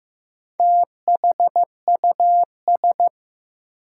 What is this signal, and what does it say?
Morse code, heard as a keyed tone, about 15 wpm, 700 Hz. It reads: THUS
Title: Sentences from Top 800 Words - New Word Drill - 15wpm